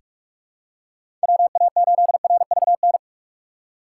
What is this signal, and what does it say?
Morse code, heard as a keyed tone, about 46 wpm, 700 Hz. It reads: WA8RVN
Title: USA Call Signs - 6x Character Spacing - 45wpm